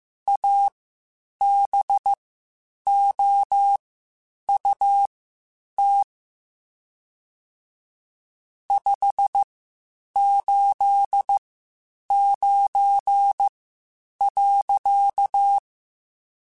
Morse code, heard as a keyed tone, about 15 wpm, spaced as for 8 wpm, 800 Hz. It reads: ABOUT 589.
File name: QSOs / QSOs-mp3s / WD0FBT 1@13